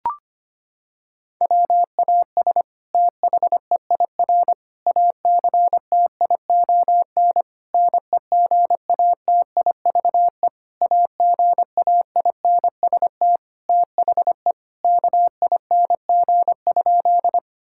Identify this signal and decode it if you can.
Morse code, heard as a keyed tone, about 25 wpm, 700 Hz. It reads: WAS THEIR ACTION NEGATIVE AGAINST THE KING?
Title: Sentences from Top 1000 Words - Review Part 6 - 25wpm